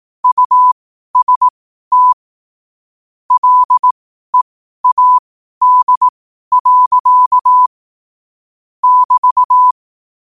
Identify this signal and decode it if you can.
Morse code, heard as a keyed tone, about 18 wpm, spaced as for 12 wpm, 1000 Hz. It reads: UST LEAD. <BT>
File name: qotdfiles / 12wpm